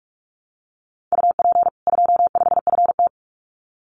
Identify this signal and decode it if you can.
Morse code, heard as a keyed tone, about 44 wpm, 700 Hz. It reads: UP 25FT
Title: Common QSO Phrases 45wpm